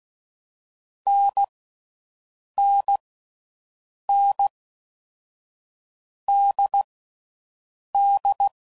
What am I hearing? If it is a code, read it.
Morse code, heard as a keyed tone, about 16 wpm, 800 Hz. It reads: N N N D D
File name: Mixed / Short Runs / NDB NDB6